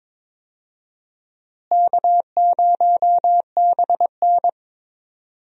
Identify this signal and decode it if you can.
Morse code, heard as a keyed tone, about 22 wpm, 700 Hz. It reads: K0BN
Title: USA Call Signs - 4 Characters - Repeated 3x - 22wpm